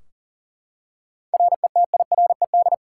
Morse code, heard as a keyed tone, about 40 wpm, 700 Hz. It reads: RETIRED
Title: Practice QSO - Exchange 3 - More Info - 40wpm